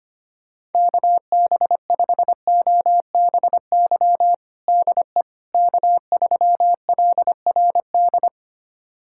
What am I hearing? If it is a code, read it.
Morse code, heard as a keyed tone, about 25 wpm, 700 Hz. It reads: KB5OBY DE K3LRD
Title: USA Callsign DE Callsign 25wpm